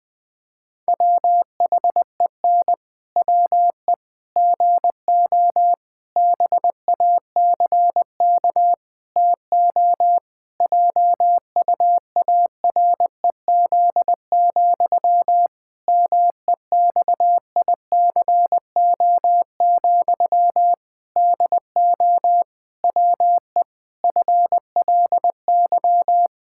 Morse code, heard as a keyed tone, about 20 wpm, 700 Hz. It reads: WHEN WE GO BACK TO JUAREZ, MEXICO, DO WE FLY
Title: Pangram Challenge v2 - 20wpm